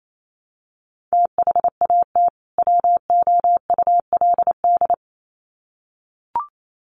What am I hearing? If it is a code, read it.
Morse code, heard as a keyed tone, about 28 wpm, 700 Hz. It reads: THAT WOULD
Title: Sets of 2 Words 28wpm